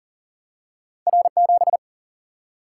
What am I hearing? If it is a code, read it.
Morse code, heard as a keyed tone, about 39 wpm, 700 Hz. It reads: R7
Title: Single-Letter Single-Number 40wpm